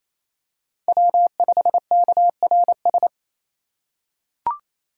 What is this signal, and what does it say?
Morse code, heard as a keyed tone, about 28 wpm, 700 Hz. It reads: W5KRS
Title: USA Call Signs - 5 Characters - 4x Character Spacing - 28wpm